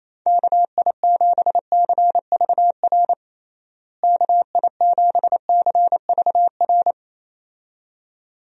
Morse code, heard as a keyed tone, about 28 wpm, 700 Hz. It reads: KI7CVR KI7CVR
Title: USA Call Signs - 6 Characters - Repeated 2x - 28wpm